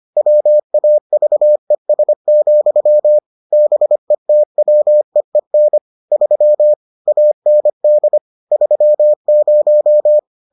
Morse code, heard as a keyed tone, about 25 wpm, 600 Hz. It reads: WAVES, BETWEEN 3 AND 30